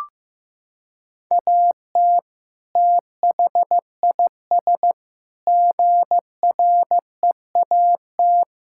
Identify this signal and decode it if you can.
Morse code, heard as a keyed tone, about 16 wpm, 700 Hz. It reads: AT THIS GREAT
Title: Sentences from Top 1600 Words 15wpm